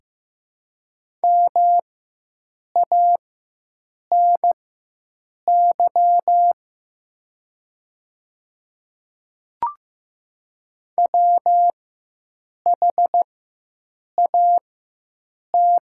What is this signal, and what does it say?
Morse code, heard as a keyed tone, about 15 wpm, spaced as for 7 wpm, 700 Hz. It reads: MANY WHAT
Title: 4x Character Spacing - Top 100 Words - 15wpm